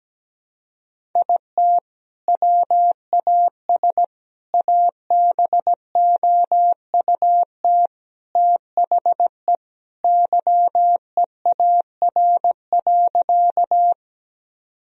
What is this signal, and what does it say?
Morse code, heard as a keyed tone, about 17 wpm, 700 Hz. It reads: IT WAS ABOUT THE YEAR.
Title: Sentences from Top 1000 Words - Review Part 1 - 17wpm